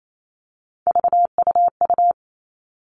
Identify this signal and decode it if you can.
Morse code, heard as a keyed tone, about 28 wpm, 700 Hz. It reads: VUU
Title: All Cut Numbers - 3-Digit 28wpm